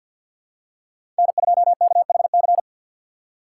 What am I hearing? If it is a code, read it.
Morse code, heard as a keyed tone, about 51 wpm, 690 Hz. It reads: N2KHC
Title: USA Call Signs - 2x Character Spacing - 50wpm